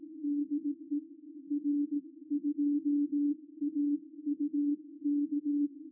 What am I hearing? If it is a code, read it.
Morse code, heard as a keyed tone, about 18 wpm, spaced as for 16 wpm, 290 Hz. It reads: DE R2AUK
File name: r2auk-g3xjs-qrp-qso-recorded-by-ruud-vos